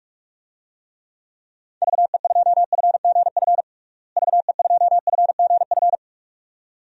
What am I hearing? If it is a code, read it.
Morse code, heard as a keyed tone, about 47 wpm, 700 Hz. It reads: VE2FGF VE2FGF
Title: DXCC-100 Call Signs - 6 Characters - Repeated 2x - 45wpm